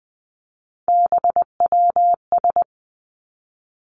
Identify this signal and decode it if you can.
Morse code, heard as a keyed tone, about 20 wpm, 700 Hz. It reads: BWS
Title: Three Challenging Letter-Number Combinations 20wpm